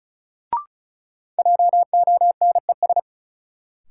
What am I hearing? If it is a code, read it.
Morse code, heard as a keyed tone, about 34 wpm, 700 Hz. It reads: JONES
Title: Sets of 5 Words - New Word Drill - 35wpm